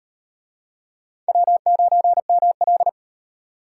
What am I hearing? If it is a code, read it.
Morse code, heard as a keyed tone, about 34 wpm, 700 Hz. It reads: W9ML